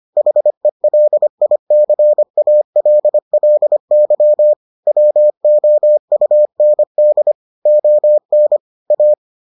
Morse code, heard as a keyed tone, about 25 wpm, 600 Hz. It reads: HELICALLY WOUND ON A